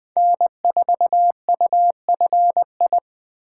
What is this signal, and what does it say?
Morse code, heard as a keyed tone, about 21 wpm, 700 Hz. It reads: N4UFI
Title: Speed Racing - USA 1x3 Call Signs 17wpm